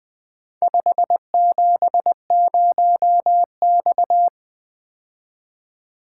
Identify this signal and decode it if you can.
Morse code, heard as a keyed tone, about 24 wpm, 700 Hz. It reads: 570X
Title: Speed-Racing - Four Letter-Number Combinations - 20wpm